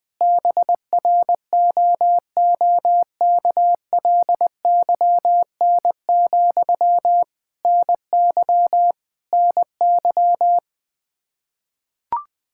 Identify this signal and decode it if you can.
Morse code, heard as a keyed tone, about 20 wpm, 700 Hz. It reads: BROOKLYN, NY NY
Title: K1USN SST QSO Practice 20wpm